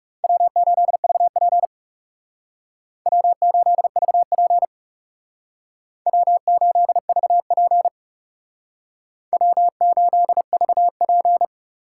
Morse code, heard as a keyed tone, about 38 wpm, 700 Hz. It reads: W8VP W8VP W8VP W8VP